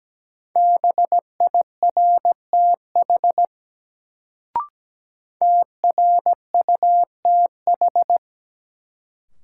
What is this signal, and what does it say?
Morse code, heard as a keyed tone, about 17 wpm, 700 Hz. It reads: BIRTH TRUTH
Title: Sentences from Top 800 Words - New Word Drill - 17wpm